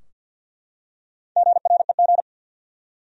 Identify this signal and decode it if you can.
Morse code, heard as a keyed tone, about 52 wpm, 700 Hz. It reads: GREG
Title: USA Common Names - Max 5 Letters - 50wpm